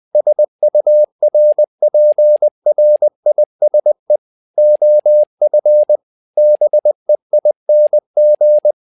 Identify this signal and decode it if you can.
Morse code, heard as a keyed tone, about 20 wpm, 600 Hz. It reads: SURPRISE OF BEING
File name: tower trouble 20wpm